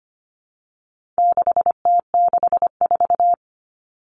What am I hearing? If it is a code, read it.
Morse code, heard as a keyed tone, about 25 wpm, 700 Hz. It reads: 6T64
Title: All Cut Numbers - 4-Digit 25wpm